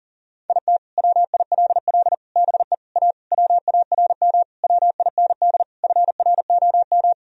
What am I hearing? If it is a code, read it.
Morse code, heard as a keyed tone, about 40 wpm, 700 Hz. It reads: IT WILL BE A WARM WIND FROM
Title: Sentences from Top 1000 Words - Review Part 3 - 40wpm